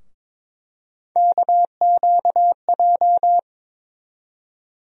Morse code, heard as a keyed tone, about 22 wpm, 700 Hz. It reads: KQJ
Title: USA Call Sign Postfixes - 3 Letters 22wpm